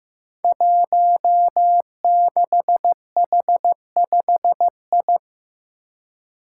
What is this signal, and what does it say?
Morse code, heard as a keyed tone, about 15 wpm, 700 Hz. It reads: 16H5I
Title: Five Challenging Letter-Number Combinations 15wpm